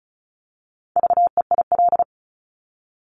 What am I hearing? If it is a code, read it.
Morse code, heard as a keyed tone, about 36 wpm, 700 Hz. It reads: VEIL
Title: Multi-letter Phonemes Repeated 3x 35wpm